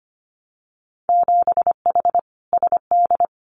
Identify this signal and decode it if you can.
Morse code, heard as a keyed tone, about 25 wpm, 700 Hz. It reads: 7H SD
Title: Winter Field Day Exchanges Encore 25wpm